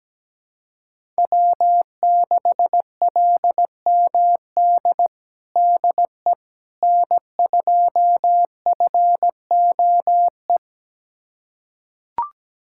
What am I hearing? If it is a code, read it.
Morse code, heard as a keyed tone, about 17 wpm, 700 Hz. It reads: W6LMD DE N2FOE
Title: USA Callsign DE Callsign - 5 Characters - 17wpm